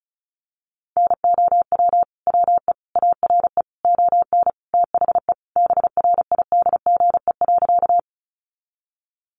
Morse code, heard as a keyed tone, about 34 wpm, 700 Hz. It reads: NOW WE ARE ON THE BRIDGE.
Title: Sentences from Top 1200 Words 35wpm